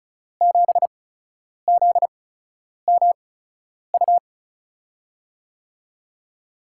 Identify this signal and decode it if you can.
Morse code, heard as a keyed tone, about 36 wpm, spaced as for 10 wpm, 700 Hz. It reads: 7ZMU
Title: USA Call Signs - 6 Characters - 8x Character Spacing - 35wpm